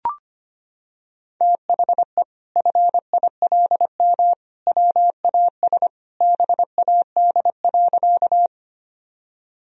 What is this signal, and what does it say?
Morse code, heard as a keyed tone, about 25 wpm, 700 Hz. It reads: THE FILM WAS BAD.
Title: Sentences from Top 1000 Words - Review Part 3 - 25wpm